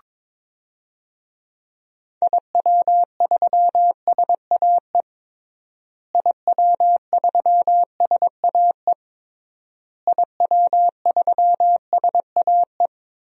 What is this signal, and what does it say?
Morse code, heard as a keyed tone, about 22 wpm, 700 Hz. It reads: IW3SAE IW3SAE IW3SAE
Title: DXCC-100 Call Signs - 6 Characters - Repeated 3x - 22wpm